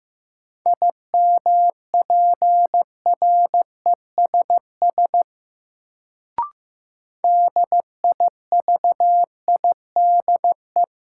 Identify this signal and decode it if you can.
Morse code, heard as a keyed tone, about 15 wpm, 700 Hz. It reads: IMPRESS DIVIDE
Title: Sentences from Top 1600 Words - New Word Drill - 15wpm